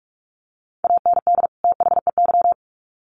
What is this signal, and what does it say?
Morse code, heard as a keyed tone, about 44 wpm, 700 Hz. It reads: AND THEY
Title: Sets of 2 Words 45wpm